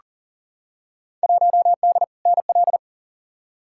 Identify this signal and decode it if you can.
Morse code, heard as a keyed tone, about 40 wpm, 700 Hz. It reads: 1D NL
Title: Field Day Exchanges Encore - 40wpm